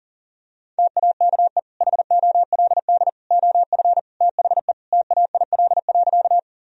Morse code, heard as a keyed tone, about 40 wpm, 700 Hz. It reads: TAKE HOLD OF THE TAIL.
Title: Sentences from Top 1000 Words - Review Part 3 - 40wpm